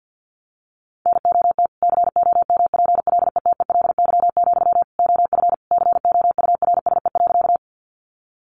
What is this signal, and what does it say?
Morse code, heard as a keyed tone, about 50 wpm, 700 Hz. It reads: NOT COMPLETELY, OF COURSE.
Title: Sentences from Top 1100 Words 50wpm